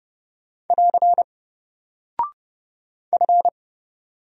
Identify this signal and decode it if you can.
Morse code, heard as a keyed tone, about 30 wpm, 700 Hz. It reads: <AR> F
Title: Single Character (Letter:Number:Punctuation:Prosign) 30wpm